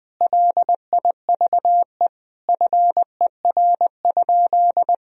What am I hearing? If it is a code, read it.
Morse code, heard as a keyed tone, about 20 wpm, 700 Hz. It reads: LIVE FER?
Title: Sentences from Top 200 Words and Abbreviations 20wpm